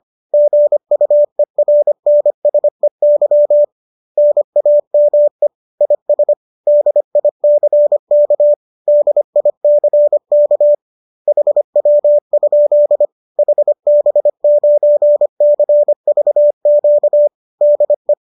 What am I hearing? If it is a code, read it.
Morse code, heard as a keyed tone, about 25 wpm, 590 Hz. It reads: GUERNSEY NAME IS DICK DICK HW? HB9CVQ DE